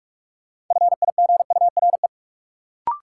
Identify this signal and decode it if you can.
Morse code, heard as a keyed tone, about 46 wpm, 700 Hz. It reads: FIGURE
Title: Sentences from Top 1400 Words - New Word Drill Alt - 45wpm